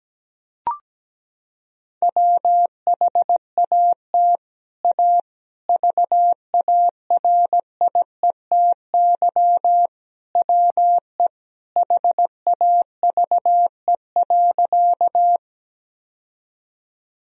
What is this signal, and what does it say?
Morse code, heard as a keyed tone, about 17 wpm, 700 Hz. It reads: WHAT A VARIETY WE HAVE.
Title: Sentences from Top 1000 Words - Review Part 4 - 17wpm